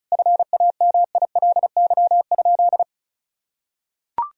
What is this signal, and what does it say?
Morse code, heard as a keyed tone, about 36 wpm, 700 Hz. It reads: FAMILY?
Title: Sentences from Top 1000 Words - Review Part 4 - 35wpm